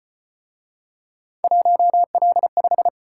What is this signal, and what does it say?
Morse code, heard as a keyed tone, about 35 wpm, 700 Hz. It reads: 1L5